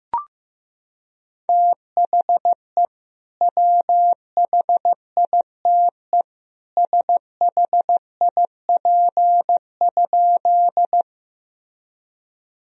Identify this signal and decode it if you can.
Morse code, heard as a keyed tone, about 16 wpm, 700 Hz. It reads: THE WHITE SHIP?
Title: Sentences from Top 1000 Words - Review Part 2 - 15wpm